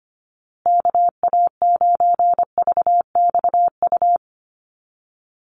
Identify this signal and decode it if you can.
Morse code, heard as a keyed tone, about 25 wpm, 700 Hz. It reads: KA9VXU
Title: USA Call Signs Repeated 2x 25wpm